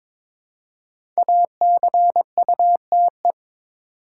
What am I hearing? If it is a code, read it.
Morse code, heard as a keyed tone, about 21 wpm, 700 Hz. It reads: ACUTE
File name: 5 Letter Words - First 3 Letters Given - 22wpm